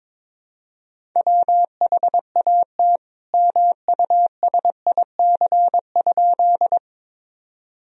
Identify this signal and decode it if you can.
Morse code, heard as a keyed tone, about 22 wpm, 700 Hz. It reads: WHAT MUSIC?
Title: Sentences from Top 1000 Words - Review Part 1 - 22wpm